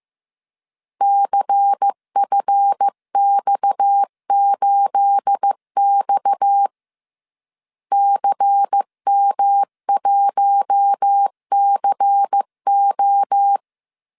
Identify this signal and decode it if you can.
Morse code, heard as a keyed tone, about 15 wpm, 800 Hz. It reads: CFX8X CM1CO
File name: CW / DET / 1 9 1500C S A 14wpm 3222